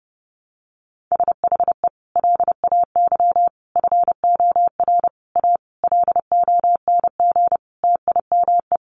Morse code, heard as a keyed tone, about 30 wpm, 700 Hz. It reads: SHE LAY FOR A LONG TIME
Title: Sentences from Top 900 Words 30wpm